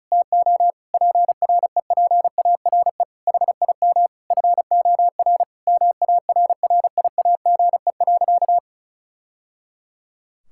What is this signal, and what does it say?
Morse code, heard as a keyed tone, about 34 wpm, 700 Hz. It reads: TO PREPARE HIM FOR MARRIAGE.
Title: Sentences from Top 1000 Words - Review Part 6 - 35wpm